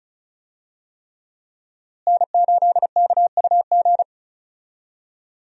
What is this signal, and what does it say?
Morse code, heard as a keyed tone, about 36 wpm, 700 Hz. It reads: N8KUG
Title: USA Call Signs - 5 Characters - Repeated 3x - 35wpm